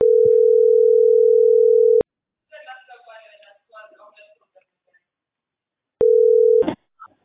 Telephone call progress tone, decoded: ringback tone

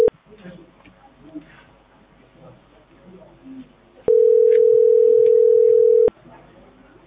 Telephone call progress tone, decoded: ringback tone

